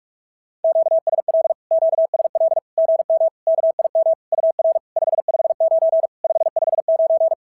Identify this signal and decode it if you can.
Morse code, heard as a keyed tone, about 48 wpm, 650 Hz. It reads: QSL QSL GM KIM UR 559 559
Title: SOTA QSO Practice 45wpm